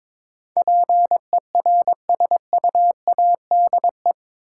Morse code, heard as a keyed tone, about 22 wpm, 700 Hz. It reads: PERSUADE
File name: practice / 22wpm / Eight Letter Words - 22wpm